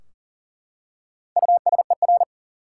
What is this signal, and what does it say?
Morse code, heard as a keyed tone, about 40 wpm, 700 Hz. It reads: USER
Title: Four Letter Words - 40wpm